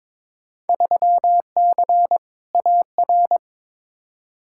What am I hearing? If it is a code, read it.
Morse code, heard as a keyed tone, about 22 wpm, 700 Hz. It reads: 3C AR
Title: Field Day Exchanges Encore - 22wpm